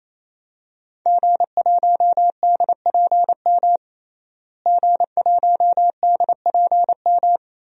Morse code, heard as a keyed tone, about 28 wpm, 700 Hz. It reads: G1DPM G1DPM
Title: DXCC-100 Call Signs - 5 Characters - Repeated 3x - 28wpm